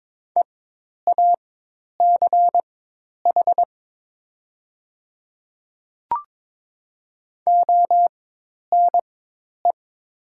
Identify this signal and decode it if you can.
Morse code, heard as a keyed tone, about 22 wpm, spaced as for 10 wpm, 700 Hz. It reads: EACH ONE
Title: 4x Character Spacing - Top 300 Words - 22wpm